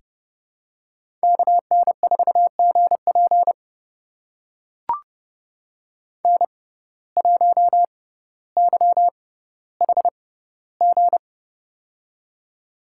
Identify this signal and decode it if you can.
Morse code, heard as a keyed tone, about 32 wpm, spaced as for 16 wpm, 700 Hz. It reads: KN4GP N1YHG
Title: USA Call Signs - 5 Characters - 6x Character Spacing - 30wpm